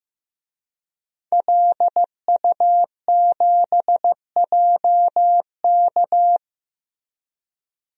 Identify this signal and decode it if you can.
Morse code, heard as a keyed tone, about 15 wpm, 700 Hz. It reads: LU7JK
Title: DXCC-100 Call Signs - 5 Characters - Repeated 2x - 15wpm